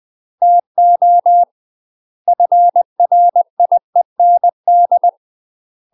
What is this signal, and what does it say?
Morse code, heard as a keyed tone, about 20 wpm, 700 Hz. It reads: TO FRIEND
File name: UK MASTER QSOs TRAINING / 20WPM / UK MASTER QSOs 0004